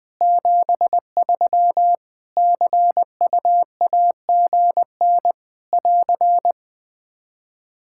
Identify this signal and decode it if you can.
Morse code, heard as a keyed tone, about 20 wpm, 700 Hz. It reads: 73 CUAGN <AR>
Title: Practice QSO - Exchange 4 - Ending - 20wpm